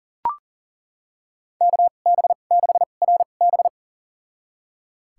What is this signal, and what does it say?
Morse code, heard as a keyed tone, about 40 wpm, spaced as for 29 wpm, 700 Hz. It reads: KB6RB
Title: USA Call Signs - 5 Characters - 2x Character Spacing - 40wpm